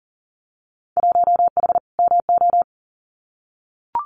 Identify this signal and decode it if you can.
Morse code, heard as a keyed tone, about 40 wpm, 700 Hz. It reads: 1H MO
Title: Winter Field Day Exchanges 40wpm